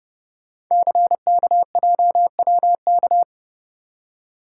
Morse code, heard as a keyed tone, about 30 wpm, 690 Hz. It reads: CKJWK
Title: Code Groups - 5 Letters - 30wpm